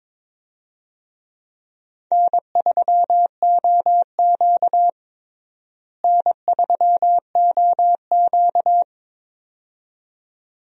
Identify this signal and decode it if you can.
Morse code, heard as a keyed tone, about 22 wpm, 700 Hz. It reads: N3OQ N3OQ
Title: USA Call Signs - 4 Characters - Repeated 2x - 22wpm